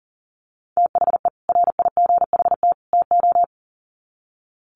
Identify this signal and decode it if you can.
Morse code, heard as a keyed tone, about 40 wpm, 700 Hz. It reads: THE RIGHT TO
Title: Sets of 3 Words 40wpm